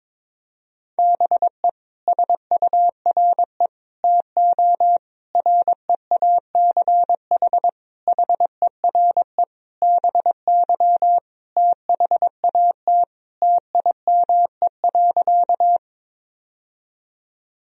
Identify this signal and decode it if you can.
Morse code, heard as a keyed tone, about 22 wpm, 700 Hz. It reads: BE SURE TO REACH HERE BY THAT TIME.
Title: Sentences from Top 1000 Words - Review Part 2 - 22wpm